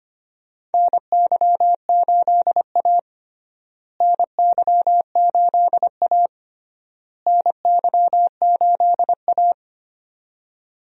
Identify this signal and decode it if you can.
Morse code, heard as a keyed tone, about 25 wpm, 700 Hz. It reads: NY8A NY8A NY8A
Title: USA Call Signs - 4 Characters - Repeated 3x - 25wpm